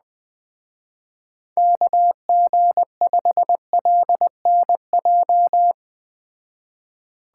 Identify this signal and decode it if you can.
Morse code, heard as a keyed tone, about 21 wpm, 700 Hz. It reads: KG5LNJ